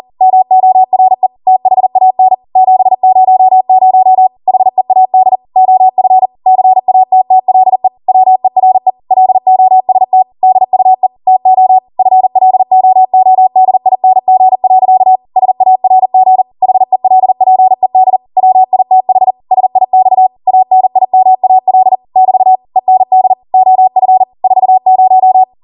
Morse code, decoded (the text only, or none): MORE THAN 700 HEAD OF CATTLE WERE LOST DUE TO FLOODING. SARG HELPED WITH SIX ANIMAL <BT> END OF 40